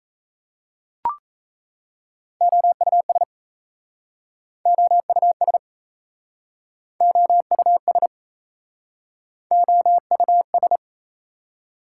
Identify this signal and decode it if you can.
Morse code, heard as a keyed tone, about 40 wpm, 700 Hz. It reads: OUS OUS OUS OUS